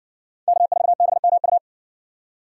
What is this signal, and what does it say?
Morse code, heard as a keyed tone, about 60 wpm, 700 Hz. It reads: K4BMU